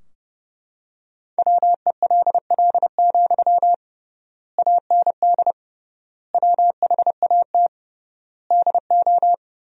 Morse code, heard as a keyed tone, about 32 wpm, 700 Hz. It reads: WELL, AND WHAT DO
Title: 3x Word-Spacing - Sentences from Top 200 Words - 30wpm